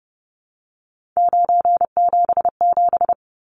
Morse code, heard as a keyed tone, about 30 wpm, 700 Hz. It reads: 977
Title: Numbers - 3-Digit 30wpm